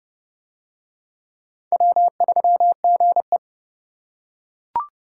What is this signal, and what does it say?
Morse code, heard as a keyed tone, about 29 wpm, 700 Hz. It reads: W3GE